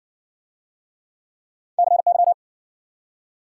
Morse code, heard as a keyed tone, about 53 wpm, 700 Hz. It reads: CY